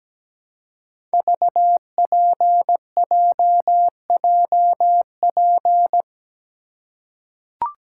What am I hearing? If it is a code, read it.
Morse code, heard as a keyed tone, about 17 wpm, 700 Hz. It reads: VPJJP
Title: Code Groups - 5 Letters - 17wpm